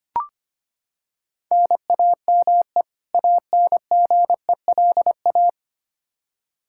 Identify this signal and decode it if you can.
Morse code, heard as a keyed tone, about 25 wpm, 700 Hz. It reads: NAME ANGELA
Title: Practice QSO - Exchange 1 - Beginning - 25wpm